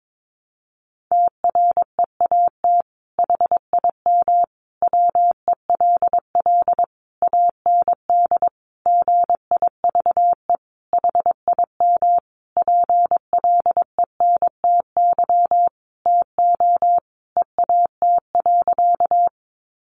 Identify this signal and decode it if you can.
Morse code, heard as a keyed tone, about 22 wpm, 700 Hz. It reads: TREAT HIM WELL AND GIVE HIM PLENTY TO EAT.
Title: Sentences from Top 1300 Words 22wpm